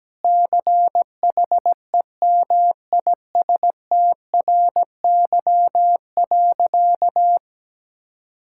Morse code, heard as a keyed tone, about 17 wpm, 700 Hz. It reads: CHEMISTRY.
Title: Sentences from Top 1000 Words - Review Part 6 - 17wpm